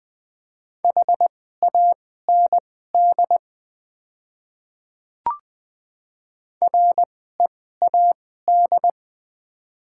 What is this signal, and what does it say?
Morse code, heard as a keyed tone, about 20 wpm, spaced as for 15 wpm, 700 Hz. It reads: HAND READ
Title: 2x Character Spacing - Top 200 Words - 20wpm